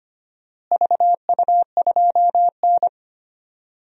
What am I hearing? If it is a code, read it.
Morse code, heard as a keyed tone, about 25 wpm, 700 Hz. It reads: VU2N
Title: DXCC-100 Call Signs - 4 Characters - Repeated 3x - 25wpm